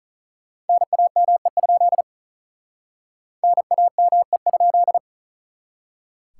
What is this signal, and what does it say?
Morse code, heard as a keyed tone, about 45 wpm, 700 Hz. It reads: NAME? NAME?